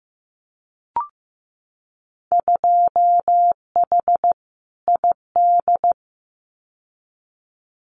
Morse code, heard as a keyed tone, about 15 wpm, 700 Hz. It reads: 2H ID
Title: Winter Field Day Exchanges 15wpm